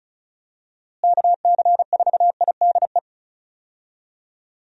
Morse code, heard as a keyed tone, about 36 wpm, 700 Hz. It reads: KC4IDE
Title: USA Call Signs - 6 Characters - Repeated 2x - 35wpm